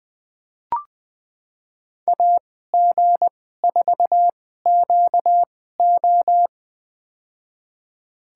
Morse code, heard as a keyed tone, about 20 wpm, spaced as for 15 wpm, 700 Hz. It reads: AG4QO
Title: USA Call Signs - 2x Character Spacing - 20wpm